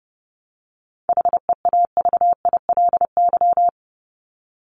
Morse code, heard as a keyed tone, about 30 wpm, 700 Hz. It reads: HEAVILY
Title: LY Suffix - Top 100 Words with Base Words 30wpm